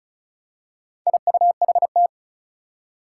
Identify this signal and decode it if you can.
Morse code, heard as a keyed tone, about 36 wpm, 700 Hz. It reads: IUHT
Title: Four Challenging Letter-Number Combinations 35wpm